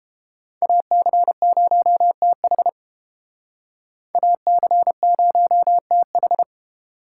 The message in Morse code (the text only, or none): AC0TH AC0TH